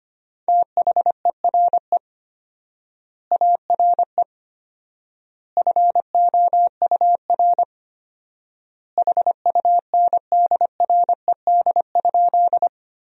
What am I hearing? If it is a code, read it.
Morse code, heard as a keyed tone, about 25 wpm, 700 Hz. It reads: THERE ARE FOUR HUNDRED?
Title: 4x Word-Spacing - Sentences from Top 400 Words - 25wpm